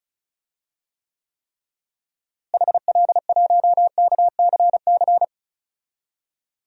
Morse code, heard as a keyed tone, about 36 wpm, 700 Hz. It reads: HL1KCC
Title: DXCC-100 Call Signs - 6 Characters - Repeated 3x - 35wpm